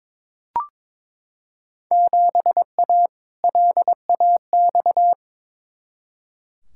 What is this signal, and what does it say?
Morse code, heard as a keyed tone, about 21 wpm, 700 Hz. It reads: 7A LAX
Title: Field Day Exchanges Encore - 22wpm